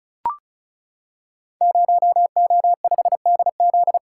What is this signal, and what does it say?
Morse code, heard as a keyed tone, about 36 wpm, 700 Hz. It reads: 0O5DZ
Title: Code Groups - 5 Letter:Numbers - 35wpm